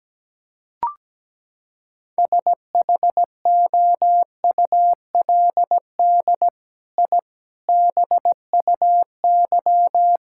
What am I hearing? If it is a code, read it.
Morse code, heard as a keyed tone, about 17 wpm, 700 Hz. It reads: SHOULD I BUY
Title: Sentences from Top 1000 Words - Review Part 5 - 17wpm